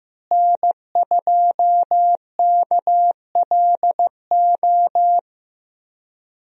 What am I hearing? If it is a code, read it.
Morse code, heard as a keyed tone, about 15 wpm, 700 Hz. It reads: N2KLO
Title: USA Call Signs - 5 Characters - Repeated 3x - 15wpm